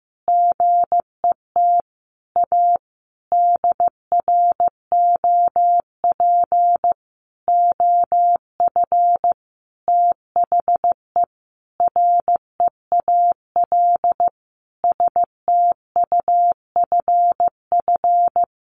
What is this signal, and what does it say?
Morse code, heard as a keyed tone, about 15 wpm, 700 Hz. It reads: GET A DROP OF THE REAL STUFF
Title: Sentences from Top 1400 Words 15wpm